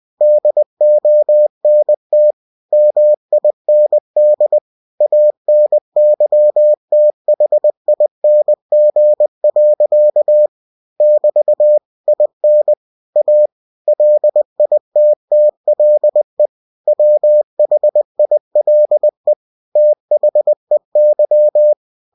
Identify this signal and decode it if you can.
Morse code, heard as a keyed tone, about 20 wpm, 600 Hz. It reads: DONT MIND ANYTHING. <BT> IN A LITTLE WHILE THEY
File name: cw-books / winnie-the-pooh / winnie-20wpm-page-0079